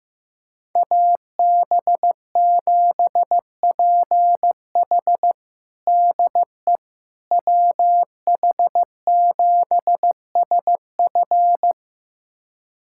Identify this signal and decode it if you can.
Morse code, heard as a keyed tone, about 15 wpm, 700 Hz. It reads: AB7PH DE WH7SF
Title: USA Callsign DE Callsign - 5 Characters - 15wpm